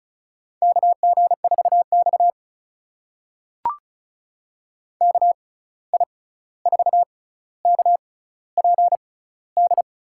Morse code, heard as a keyed tone, about 36 wpm, spaced as for 16 wpm, 700 Hz. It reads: KG4X KI4KPD
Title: USA Call Signs - 6x Character Spacing - 35wpm